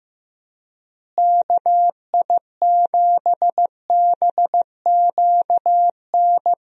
Morse code, heard as a keyed tone, about 15 wpm, 700 Hz. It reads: KI7BQN
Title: USA Call Signs - 6 Characters - Repeated 3x - 15wpm